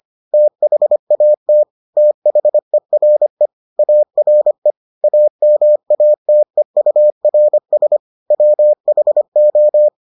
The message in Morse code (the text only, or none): THAT THERE ARE AMATEURS WHO